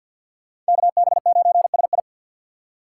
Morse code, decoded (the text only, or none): KB9SI